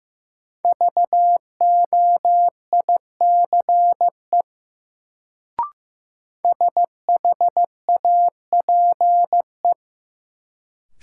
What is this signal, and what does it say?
Morse code, heard as a keyed tone, about 15 wpm, 700 Hz. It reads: VOICE SHAPE
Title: Sentences from Top 500 Words - New Word Drill - 15wpm